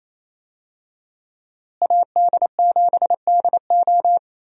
Morse code, heard as a keyed tone, about 28 wpm, 700 Hz. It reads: AD7DO
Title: USA Call Signs - 5 Characters - Repeated 3x - 28wpm